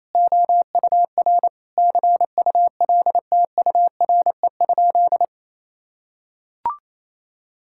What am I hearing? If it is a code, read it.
Morse code, heard as a keyed tone, about 28 wpm, 700 Hz. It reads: OUR CULTURE?
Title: Sentences from Top 1000 Words - Review Part 6 - 28wpm